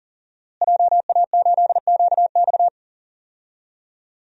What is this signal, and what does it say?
Morse code, heard as a keyed tone, about 40 wpm, 700 Hz. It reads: JA8QX
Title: DXCC-100 Call Signs - 5 Characters - Repeated 2x - 40wpm